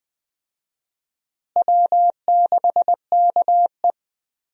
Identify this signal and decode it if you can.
Morse code, heard as a keyed tone, about 20 wpm, 700 Hz. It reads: W6KE
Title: USA Call Signs - 4 Characters - Repeated 3x - 20wpm